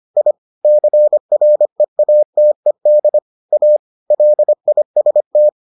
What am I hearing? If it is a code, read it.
Morse code, heard as a keyed tone, about 25 wpm, 600 Hz. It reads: I CREATED A LIST